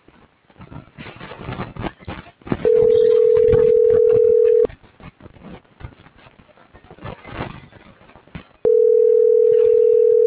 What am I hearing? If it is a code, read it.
Ringback tone (North America), 440 + 480 Hz.